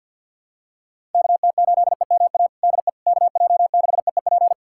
Morse code, heard as a keyed tone, about 50 wpm, 700 Hz. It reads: KT8EMA DE KJ6EEP
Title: USA Callsign DE Callsign 50wpm